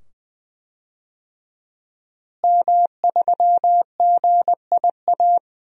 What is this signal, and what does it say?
Morse code, heard as a keyed tone, about 20 wpm, 700 Hz. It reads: M3GIA
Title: DXCC-100 Call Signs - 5 Characters - Repeated 2x - 20wpm